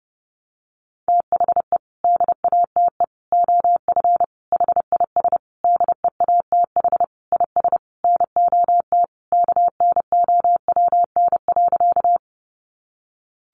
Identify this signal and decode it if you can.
Morse code, heard as a keyed tone, about 30 wpm, 700 Hz. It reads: THE DATE OF HIS DEATH IS NOT KNOWN.
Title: Sentences from Top 1100 Words 30wpm